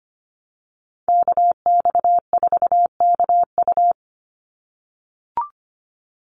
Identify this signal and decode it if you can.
Morse code, heard as a keyed tone, about 25 wpm, 700 Hz. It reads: KX4KU
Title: Call Signs - 25wpm